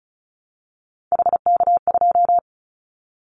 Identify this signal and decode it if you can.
Morse code, heard as a keyed tone, about 34 wpm, 700 Hz. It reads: HK2
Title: Callsign Prefixes for Top 100 DXCC - 35wpm